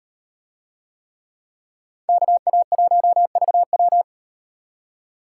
Morse code, heard as a keyed tone, about 38 wpm, 700 Hz. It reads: KA1VW